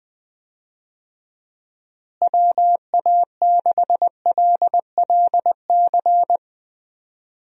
Morse code, heard as a keyed tone, about 21 wpm, 700 Hz. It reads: WA6LLC
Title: USA Call Signs - 6 Characters - Repeated 3x - 20wpm